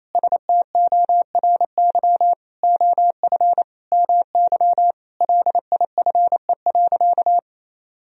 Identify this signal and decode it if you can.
Morse code, heard as a keyed tone, about 27 wpm, 700 Hz. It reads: STORY OF MY LIFE.
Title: Sentences from Top 1000 Words - Review Part 1 - 28wpm